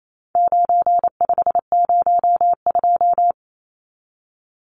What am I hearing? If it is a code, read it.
Morse code, heard as a keyed tone, about 28 wpm, 700 Hz. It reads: 9502
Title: Numbers - 4-Digit 28wpm